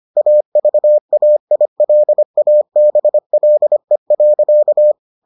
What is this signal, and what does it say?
Morse code, heard as a keyed tone, about 25 wpm, 600 Hz. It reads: AVAILABLE.